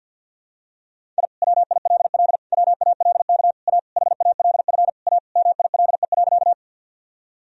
Morse code, heard as a keyed tone, about 52 wpm, 700 Hz. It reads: I WILL WALK A HALF A MILE.
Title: Sentences from Top 1000 Words - Review Part 1 - 50wpm